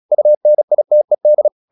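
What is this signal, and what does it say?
Morse code, heard as a keyed tone, about 36 wpm, 600 Hz. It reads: UNITED
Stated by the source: Morse practice at 36 wpm, text of Ken Follett's novel Never, from CW med Gnister.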